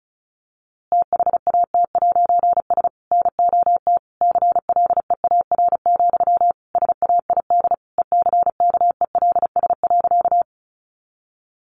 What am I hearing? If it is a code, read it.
Morse code, heard as a keyed tone, about 34 wpm, 700 Hz. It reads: THAT'S NOT CLEAR, SAID ECKELS.
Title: A Sound Of Thunder 35wpm